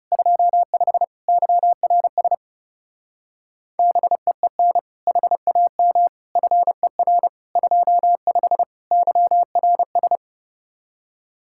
Morse code, heard as a keyed tone, about 36 wpm, 700 Hz. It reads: 25 YRS BEEN HAM FER 25 YRS
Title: Speed-Racing Common QSO Phrases Encore - 30wpm